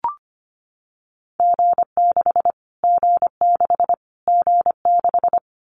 Morse code, heard as a keyed tone, about 25 wpm, 700 Hz. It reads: G6 G6 G6
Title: Callsign Prefixes for Top 25 DXCC - Repeated 3x - 25wpm